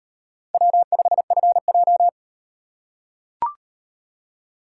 Morse code, heard as a keyed tone, about 38 wpm, 700 Hz. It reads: W5FJ